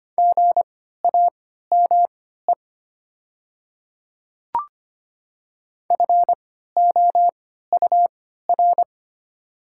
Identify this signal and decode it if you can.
Morse code, heard as a keyed tone, about 25 wpm, spaced as for 14 wpm, 700 Hz. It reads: GAME FOUR
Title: 3x Character Spacing - Top 500 Words - 25wpm